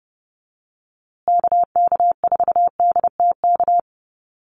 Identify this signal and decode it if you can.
Morse code, heard as a keyed tone, about 30 wpm, 700 Hz. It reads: KK4DTK
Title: USA Call Signs Repeated 2x 30wpm